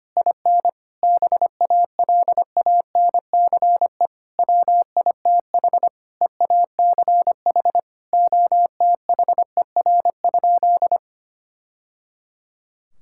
Morse code, heard as a keyed tone, about 25 wpm, 700 Hz. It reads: IN BALANCE WITH EACH OTHER?
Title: Sentences from Top 1000 Words - Review Part 6 - 25wpm